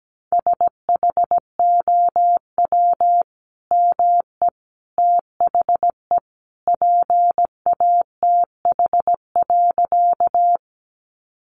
Morse code, heard as a keyed tone, about 17 wpm, 700 Hz. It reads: SHOW ME THE PATH.
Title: Sentences from Top 1400 Words 17wpm